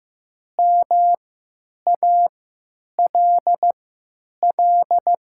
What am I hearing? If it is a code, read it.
Morse code, heard as a keyed tone, about 15 wpm, spaced as for 9 wpm, 700 Hz. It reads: MALL
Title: 3x Character Spacing - Top 300 Words - 15wpm